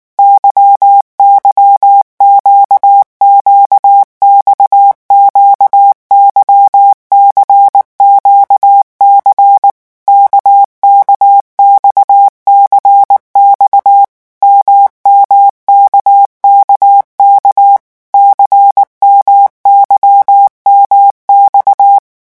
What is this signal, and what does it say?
Morse code, heard as a keyed tone, about 19 wpm, 800 Hz. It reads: YYQQX QYCQC KKXCX MMKKK CMYMX
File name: revision1-vitesse 18 mots